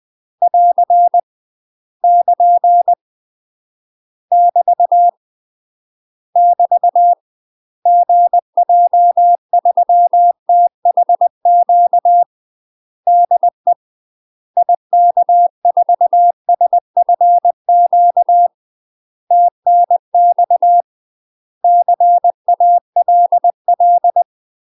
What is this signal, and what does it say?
Morse code, heard as a keyed tone, about 20 wpm, 700 Hz. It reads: <AR> <KN> <BT> <BT> GJ3THQ DE IK4SFQ TNX CALL
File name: UK MASTER QSOs TRAINING/20WPM